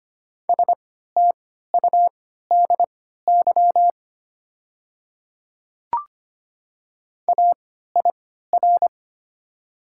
Morse code, heard as a keyed tone, about 25 wpm, spaced as for 14 wpm, 700 Hz. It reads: STUDY AIR
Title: 3x Character Spacing - Top 300 Words - 25wpm